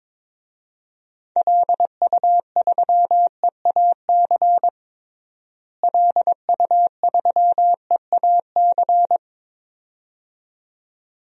Morse code, heard as a keyed tone, about 22 wpm, 700 Hz. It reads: LU3EAC LU3EAC
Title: DXCC-100 Call Signs - 6 Characters - Repeated 2x - 22wpm